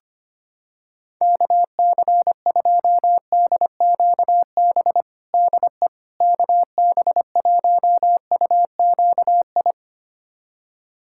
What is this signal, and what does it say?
Morse code, heard as a keyed tone, about 25 wpm, 700 Hz. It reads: KC2DQB DE KB1UQI
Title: USA Callsign DE Callsign 25wpm